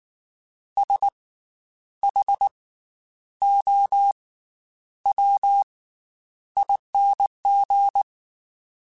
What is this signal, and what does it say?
Morse code, heard as a keyed tone, about 19 wpm, 800 Hz. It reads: S H O W ING
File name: Special INGs / Special INGs-mp3s / Slow ing @18